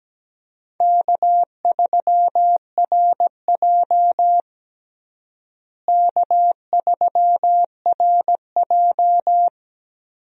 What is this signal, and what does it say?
Morse code, heard as a keyed tone, about 17 wpm, 700 Hz. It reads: K3RJ K3RJ
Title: USA Call Signs - 4 Characters - Repeated 3x - 17wpm